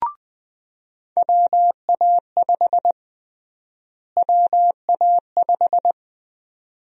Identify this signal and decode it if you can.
Morse code, heard as a keyed tone, about 20 wpm, 700 Hz. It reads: WA5 WA5
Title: USA Call Sign Prefixes - Repeated 2x - 20wpm